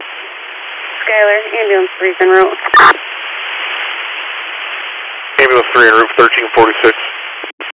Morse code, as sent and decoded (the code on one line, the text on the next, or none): none